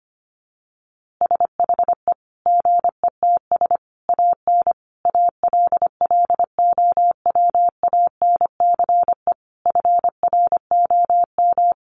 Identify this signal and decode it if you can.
Morse code, heard as a keyed tone, about 25 wpm, 700 Hz. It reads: SHE GETS AN ALLOWANCE FROM
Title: Easy Phrases 25wpm